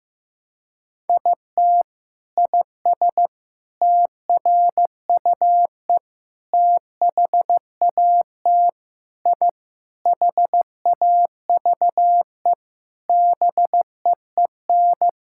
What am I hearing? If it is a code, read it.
Morse code, heard as a keyed tone, about 15 wpm, 700 Hz. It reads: IT IS TRUE THAT I HAVE BEEN
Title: Sentences from Top 1000 Words - Review Part 6 - 15wpm